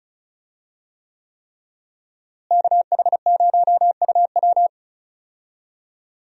Morse code, heard as a keyed tone, about 35 wpm, 700 Hz. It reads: KH0UW
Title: USA Call Signs - 5 Characters - Repeated 3x - 35wpm